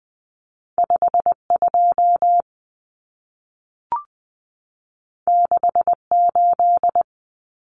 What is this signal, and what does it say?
Morse code, heard as a keyed tone, about 20 wpm, 700 Hz. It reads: 52 68
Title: Common Cut Numbers - 2-Digit 20wpm